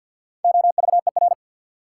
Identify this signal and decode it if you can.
Morse code, heard as a keyed tone, about 53 wpm, 700 Hz. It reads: OVER
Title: Sets of 2 Words - New Word Drill - 50wpm